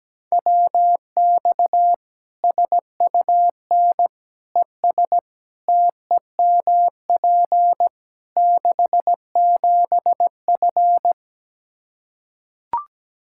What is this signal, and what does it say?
Morse code, heard as a keyed tone, about 17 wpm, 700 Hz. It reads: WX SUN ES TEMP 67F
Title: Practice QSO - Exchange 2 - Info - 17wpm